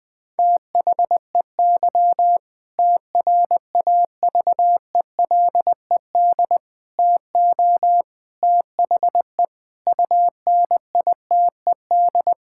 Morse code, heard as a keyed tone, about 21 wpm, 700 Hz. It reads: THEY TRAVELED TO THE UNITED
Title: Sentences with 2-Word Phrases - 20wpm